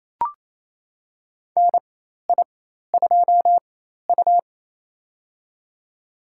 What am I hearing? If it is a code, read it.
Morse code, heard as a keyed tone, about 28 wpm, spaced as for 13 wpm, 700 Hz. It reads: NI2U
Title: USA Call Signs - 4 Characters - 4x Character Spacing - 28wpm